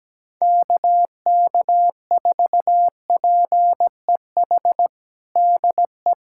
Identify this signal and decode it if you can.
Morse code, heard as a keyed tone, about 17 wpm, 700 Hz. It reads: KK4PEH DE
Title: USA Callsign DE Callsign - 6 Characters - 17wpm